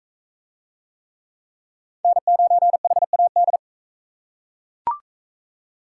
Morse code, heard as a keyed tone, about 40 wpm, 700 Hz. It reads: N9HAD